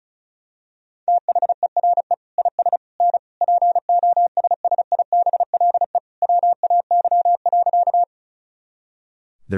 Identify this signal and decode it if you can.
Morse code, heard as a keyed tone, about 36 wpm, 700 Hz. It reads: THERE IS N POSSIBLE WAY.
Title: Sentences from Top 500 Words and Abbreviations 35wpm